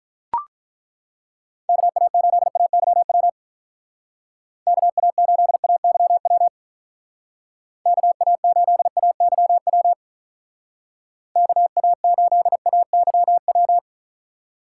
Morse code, heard as a keyed tone, about 42 wpm, 700 Hz. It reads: KA8AYW KA8AYW KA8AYW KA8AYW
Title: Speed Racing - USA 2x3 Call Signs 35wpm